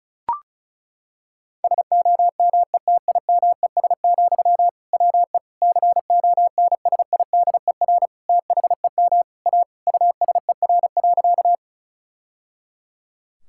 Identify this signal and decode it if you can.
Morse code, heard as a keyed tone, about 34 wpm, 690 Hz. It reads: SOMETIMES, WE CONSIDER THEM A USER.
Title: Sentences from Top 1000 Words - Review Part 5 - 35wpm